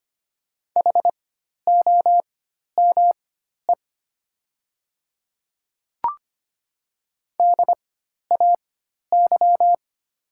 Morse code, heard as a keyed tone, about 25 wpm, spaced as for 12 wpm, 700 Hz. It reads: HOME DAY
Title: 4x Character Spacing - Top 300 Words - 25wpm